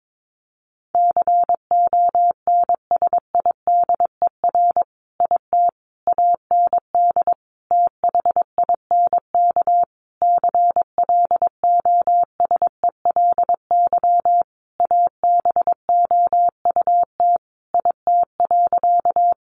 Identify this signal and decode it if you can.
Morse code, heard as a keyed tone, about 22 wpm, 700 Hz. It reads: CONSIDER IT AND THINK CLOSELY ABOUT IT.
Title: Sentences from Top 1400 Words 22wpm